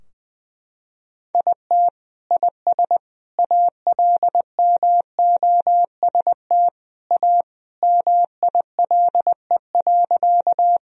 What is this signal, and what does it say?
Morse code, heard as a keyed tone, about 21 wpm, 700 Hz. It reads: IT IS ALMOST A MILE.
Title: Sentences from Top 1000 Words - Review Part 4 - 20wpm